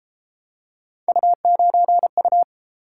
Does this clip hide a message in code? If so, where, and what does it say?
Yes: Morse code from 1.1 to 2.4 s, heard as a keyed tone, about 32 wpm, 700 Hz. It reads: U9U